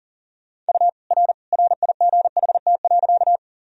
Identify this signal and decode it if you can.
Morse code, heard as a keyed tone, about 40 wpm, 700 Hz. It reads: U R RIGHT.
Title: Sentences from Top 200 Words and Abbreviations 40wpm